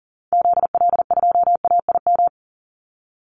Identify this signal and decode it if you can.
Morse code, heard as a keyed tone, about 40 wpm, 700 Hz. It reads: ZL2AIM
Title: Call Signs Repeated 3x - 40wpm